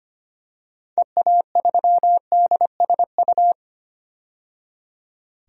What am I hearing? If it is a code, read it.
Morse code, heard as a keyed tone, about 25 wpm, 700 Hz. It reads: EA3DSU
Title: DXCC-100 Call Signs - 6 Characters - 25wpm